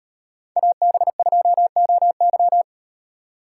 Morse code, heard as a keyed tone, about 38 wpm, 700 Hz. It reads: AB2OY